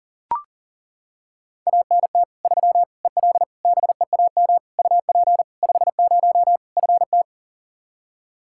Morse code, heard as a keyed tone, about 37 wpm, 700 Hz. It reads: ANT 3 EL BEAM UP 50 FT
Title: Practice QSO - Exchange 2 - Info - 40wpm